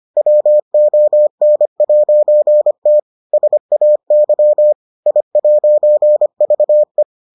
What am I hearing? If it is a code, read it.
Morse code, heard as a keyed tone, about 25 wpm, 600 Hz. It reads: WON'T SAY I'VE